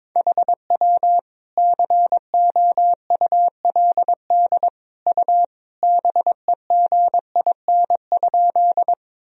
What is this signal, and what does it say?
Morse code, heard as a keyed tone, about 22 wpm, 700 Hz. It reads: HW COULD U BEGIN?
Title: Sentences from Top 300 Words and Abbreviations 22wpm